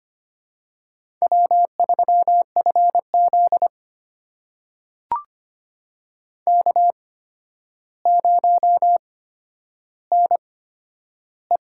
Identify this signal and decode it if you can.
Morse code, heard as a keyed tone, about 25 wpm, spaced as for 11 wpm, 700 Hz. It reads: W3FZ K0NE
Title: USA Call Signs - 4 Characters - 8x Character Spacing - 25wpm